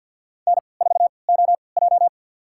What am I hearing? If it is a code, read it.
Morse code, heard as a keyed tone, about 52 wpm, spaced as for 29 wpm, 700 Hz. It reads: N4OJ
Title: USA Call Signs - 3x Character Spacing - 50wpm